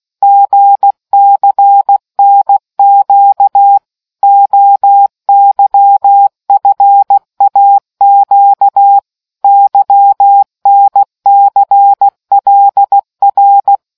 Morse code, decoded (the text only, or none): GCNQ OYFAQ YNCLR